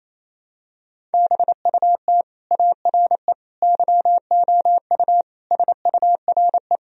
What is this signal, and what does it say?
Morse code, heard as a keyed tone, about 28 wpm, 700 Hz. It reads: BUT ARE YOU SURE
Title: Sentences from Top 1000 Words - Review Part 1 - 28wpm